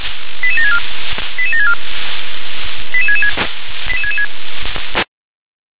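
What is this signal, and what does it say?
Five-tone selective call (ZVEI), heard as quick runs of five tones, tones of about 80 ms. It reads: 88764 88764 88666
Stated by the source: ZVEI five-tone call received on SDR.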